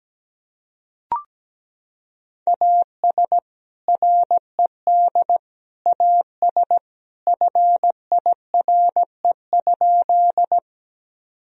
Morse code, heard as a keyed tone, about 17 wpm, 700 Hz. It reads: AS RED AS FIRE?
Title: Sentences from Top 1000 Words - Review Part 2 - 17wpm